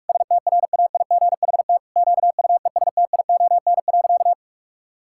Morse code, heard as a keyed tone, about 46 wpm, 700 Hz. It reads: STRAIGHT QUESTION.
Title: Sentences from Top 1000 Words - Review Part 6 - 45wpm